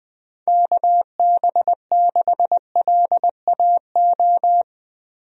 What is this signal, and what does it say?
Morse code, heard as a keyed tone, about 20 wpm, 700 Hz. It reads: KB6LAO
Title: USA Call Signs - 6 Characters - 20wpm